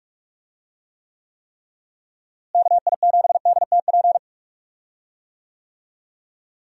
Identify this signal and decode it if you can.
Morse code, heard as a keyed tone, about 44 wpm, 700 Hz. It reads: KI7DTP
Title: USA Call Signs - 6 Characters - Repeated 3x - 45wpm